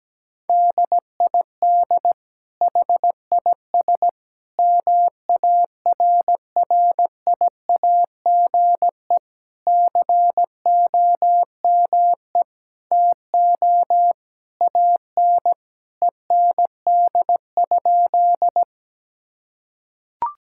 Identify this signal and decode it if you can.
Morse code, heard as a keyed tone, about 17 wpm, 700 Hz. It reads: DID HIS MARRIAGE COME TO AN END?
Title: Sentences from Top 1000 Words - Review Part 5 - 17wpm